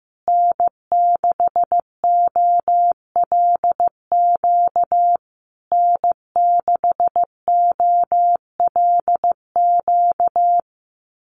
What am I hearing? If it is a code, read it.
Morse code, heard as a keyed tone, about 15 wpm, 700 Hz. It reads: N6OLQ N6OLQ
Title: USA Call Signs Repeated 3x 15wpm